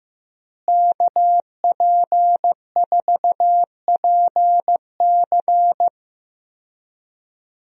Morse code, heard as a keyed tone, about 15 wpm, 700 Hz. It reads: KP4PC
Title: DXCC-100 Call Signs - 5 Characters - Repeated 3x - 15wpm